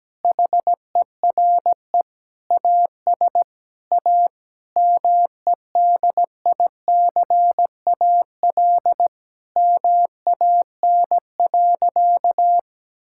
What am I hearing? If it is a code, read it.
Morse code, heard as a keyed tone, about 17 wpm, 700 Hz. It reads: HERE AS A MEDICAL MAN.
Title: Sentences from Top 1000 Words - Review Part 5 - 17wpm